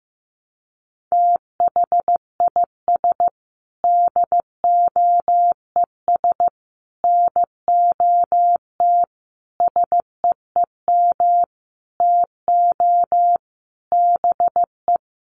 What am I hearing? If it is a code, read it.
Morse code, heard as a keyed tone, about 15 wpm, 700 Hz. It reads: THIS DOES NOT SEEM TO BE
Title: Sentences from Top 1400 Words 15wpm